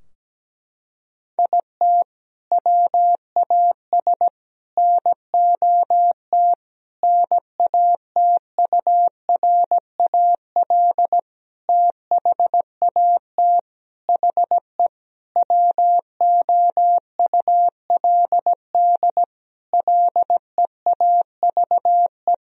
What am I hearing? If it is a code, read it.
Morse code, heard as a keyed tone, about 17 wpm, 700 Hz. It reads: IT WAS NOT NATURAL THAT HE WOULD LEAVE
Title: Sentences from Top 1000 Words - Review Part 4 - 17wpm